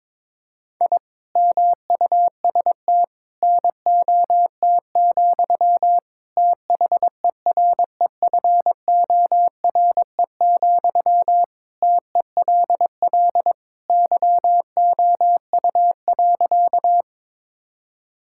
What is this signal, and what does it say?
Morse code, heard as a keyed tone, about 22 wpm, 700 Hz. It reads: I MUST NOT, THEREFORE, TELL YOU.
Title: Sentences from Top 1000 Words - Review Part 5 - 22wpm